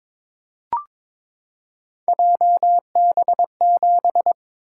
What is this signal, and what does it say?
Morse code, heard as a keyed tone, about 22 wpm, 700 Hz. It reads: JB7
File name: practice / 22wpm / Three Letter-Number Combinations - 22wpm